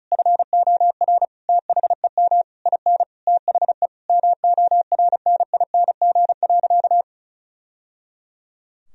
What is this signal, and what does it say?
Morse code, heard as a keyed tone, about 36 wpm, 700 Hz. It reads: FOR THEM IN THE MORNING.
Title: Sentences from Top 1000 Words - Review Part 2 - 35wpm